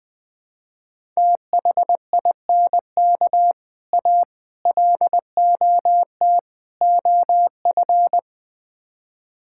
Morse code, heard as a keyed tone, about 20 wpm, 700 Hz. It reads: THINK A LOT OF
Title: Build A Phrase - 5-Word Phrases - 20wpm